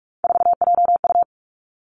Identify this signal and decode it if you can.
Morse code, heard as a keyed tone, about 46 wpm, 700 Hz. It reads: 4JU
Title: Three Challenging Letter-Number Combinations 45wpm